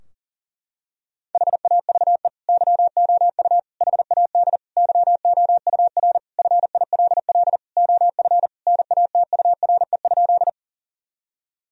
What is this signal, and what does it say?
Morse code, heard as a keyed tone, about 40 wpm, 700 Hz. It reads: HAVE YOU HAD YOUR FILL OF NATURE?
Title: Sentences from Top 1000 Words - Review Part 3 - 40wpm